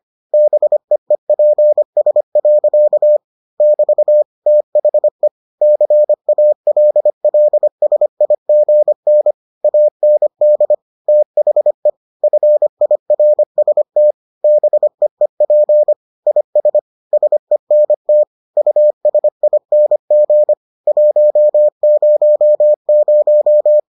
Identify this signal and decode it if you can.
Morse code, heard as a keyed tone, about 25 wpm, 600 Hz. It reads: BEEPS. <BT> THE CALLSIGN AND THE FIRST BEEP IS SENT USING 100